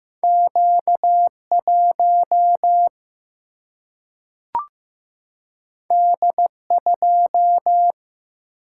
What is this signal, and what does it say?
Morse code, heard as a keyed tone, about 15 wpm, 700 Hz. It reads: Q 1 D 2
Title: Single-Letter Single-Number 15wpm